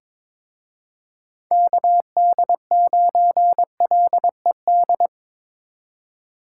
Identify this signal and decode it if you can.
Morse code, heard as a keyed tone, about 27 wpm, 700 Hz. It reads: KD9LED
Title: Speed Racing - USA Call Signs 22wpm